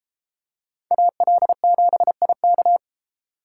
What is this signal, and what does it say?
Morse code, heard as a keyed tone, about 31 wpm, 700 Hz. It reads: AL7IK